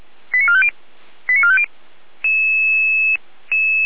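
Five-tone selective call (ZVEI), heard as quick runs of five tones, tones of about 70 ms. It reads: 89450 89450
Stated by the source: ZVEI five-tone selective call received on an SDR.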